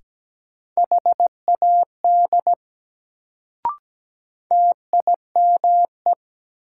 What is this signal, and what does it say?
Morse code, heard as a keyed tone, about 17 wpm, 700 Hz. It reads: HAD TIME
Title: Sentences from Top 100 Words - New Word Drill - 17wpm